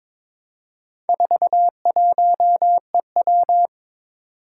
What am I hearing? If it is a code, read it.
Morse code, heard as a keyed tone, about 22 wpm, 700 Hz. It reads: 41EW
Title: Four Challenging Letter-Number Combinations 22wpm